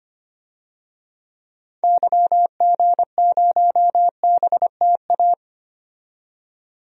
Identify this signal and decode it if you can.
Morse code, heard as a keyed tone, about 25 wpm, 700 Hz. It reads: YG0BTA
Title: DXCC-100 Call Signs - 6 Characters - 25wpm